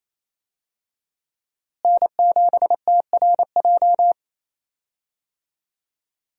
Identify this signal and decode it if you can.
Morse code, heard as a keyed tone, about 28 wpm, 700 Hz. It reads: N7TRJ
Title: USA Call Signs - 5 Characters - Repeated 3x - 28wpm